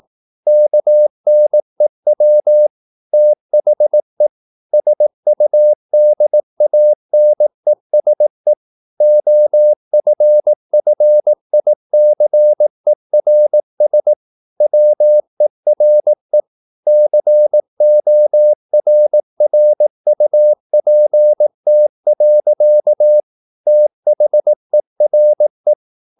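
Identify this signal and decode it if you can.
Morse code, heard as a keyed tone, about 18 wpm, 600 Hz. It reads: KNEW THE SUDANESE OFFICERS WERE CORRUPT. THERE
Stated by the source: Morse practice at 18 wpm, English text, from CW med Gnister.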